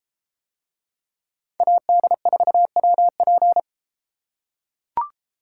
Morse code, heard as a keyed tone, about 33 wpm, 700 Hz. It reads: AD4WP